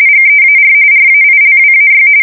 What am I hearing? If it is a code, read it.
RTTY, about 45.45 baud. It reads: GEAR RAILWAY